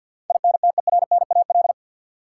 Morse code, heard as a keyed tone, about 53 wpm, 700 Hz. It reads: INTERNAL
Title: Sentences from Top 1400 Words - New Word Drill - 50wpm